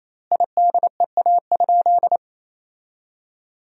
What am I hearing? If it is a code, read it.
Morse code, heard as a keyed tone, about 27 wpm, 700 Hz. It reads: IDEA?
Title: 4x Word-Spacing - Sentences from Top 400 Words - 28wpm